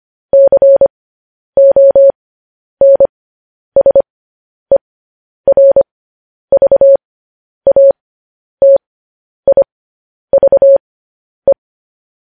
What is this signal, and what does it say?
Morse code, heard as a keyed tone, about 25 wpm, spaced as for 10 wpm, 550 Hz. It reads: CONSERVATIVE
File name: podcast / News headlines-1x-25x10